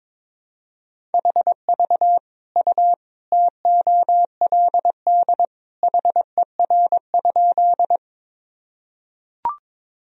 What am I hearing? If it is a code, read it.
Morse code, heard as a keyed tone, about 22 wpm, 700 Hz. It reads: HV U TOLD HER?
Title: Sentences from Top 400 Words and Abbreviations 22wpm